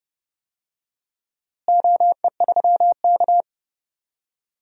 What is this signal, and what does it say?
Morse code, heard as a keyed tone, about 30 wpm, 700 Hz. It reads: OE3K
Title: DXCC-100 Call Signs - 4 Characters - Repeated 3x - 30wpm